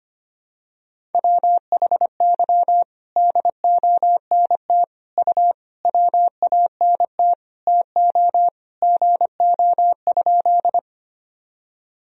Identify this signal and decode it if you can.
Morse code, heard as a keyed tone, about 25 wpm, 700 Hz. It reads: WHY DONT U WANT TO GO?
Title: Sentences from Top 300 Words and Abbreviations 25wpm